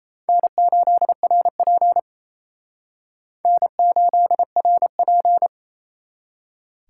Morse code, heard as a keyed tone, about 32 wpm, 700 Hz. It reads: N8RP N8RP
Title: Speed Racing - USA 1x2 Call Signs 28wpm